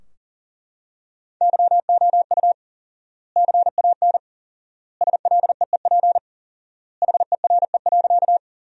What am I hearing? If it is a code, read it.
Morse code, heard as a keyed tone, about 40 wpm, 700 Hz. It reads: YOU CAN SLEEP HERE.
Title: 4x Word-Spacing - Sentences from Top 500 Words - 40wpm